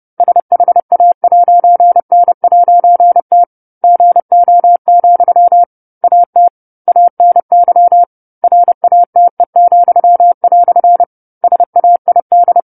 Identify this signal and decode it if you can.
Morse code, heard as a keyed tone, about 30 wpm, 700 Hz. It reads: SHA'N'T GO, AT ANY RATE," SAID